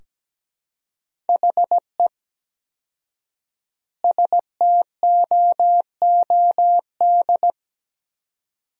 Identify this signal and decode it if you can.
Morse code, heard as a keyed tone, about 17 wpm, 700 Hz. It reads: HE STOOD
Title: 4x Word-Spacing - Sentences from Top 500 Words - 17wpm